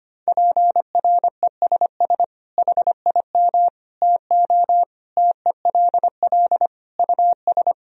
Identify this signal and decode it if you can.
Morse code, heard as a keyed tone, about 25 wpm, 700 Hz. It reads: PRESS HIM TO TELL US
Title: Sentences from Top 1000 Words - Review Part 1 - 25wpm